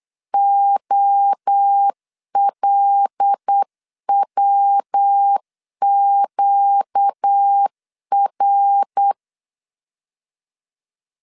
Morse code, heard as a keyed tone, about 8 wpm, 800 Hz. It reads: OLWQR
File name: CW / DET / 1 9 1500C S A 8wpm 5637